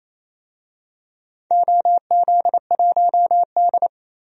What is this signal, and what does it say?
Morse code, heard as a keyed tone, about 28 wpm, 700 Hz. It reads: OZ1D
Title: DXCC-100 Call Signs - 4 Characters - Repeated 2x - 28wpm